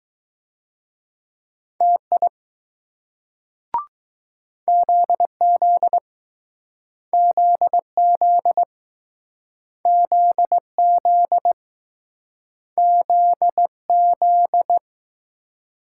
Morse code, heard as a keyed tone, about 20 wpm, 700 Hz. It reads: TI ZZ ZZ ZZ ZZ